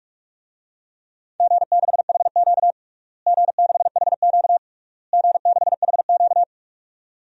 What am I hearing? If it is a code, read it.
Morse code, heard as a keyed tone, about 48 wpm, 700 Hz. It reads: G6HQ G6HQ G6HQ
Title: DXCC-100 Call Signs - 4 Characters - Repeated 3x - 45wpm